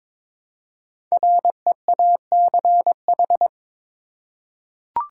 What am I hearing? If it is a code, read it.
Morse code, heard as a keyed tone, about 23 wpm, 700 Hz. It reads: REACH
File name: 5 Letter Words - First 3 Letters Given - 22wpm